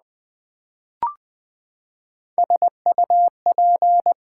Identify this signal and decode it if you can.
Morse code, heard as a keyed tone, about 20 wpm, 700 Hz. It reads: SUP
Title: Build A Word - 3 Letters At A Time - 9-Letter Words - 20wpm